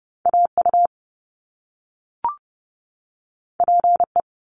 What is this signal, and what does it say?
Morse code, heard as a keyed tone, about 30 wpm, 700 Hz. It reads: AU PE
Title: Top Two Letter Combinations in Words 30wpm